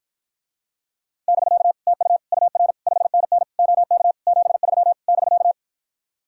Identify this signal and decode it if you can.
Morse code, heard as a keyed tone, about 52 wpm, 700 Hz. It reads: <BK> TU UR 5NN OK 73 <BK>